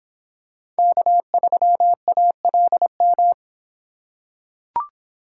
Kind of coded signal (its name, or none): Morse code